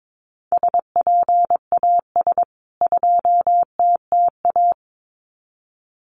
Morse code, heard as a keyed tone, about 22 wpm, 700 Hz. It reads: SPAS 2TTA
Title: CWT Exchange 22wpm